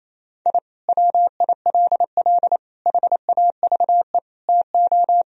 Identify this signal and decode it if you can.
Morse code, heard as a keyed tone, about 28 wpm, 700 Hz. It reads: I WILL HAVE TO
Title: Sentences from Top 1000 Words - Review Part 1 - 28wpm